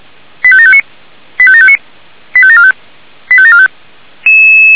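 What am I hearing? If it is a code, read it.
Five-tone selective call (ZVEI), heard as quick runs of five tones, tones of about 70 ms. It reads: 85759 85759 85745 85745
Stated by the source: ZVEI five-tone selective call received on SDR.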